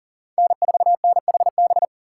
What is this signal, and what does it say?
Morse code, heard as a keyed tone, about 40 wpm, 700 Hz. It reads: N4NHB
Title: USA Call Signs - 6x Character Spacing - 40wpm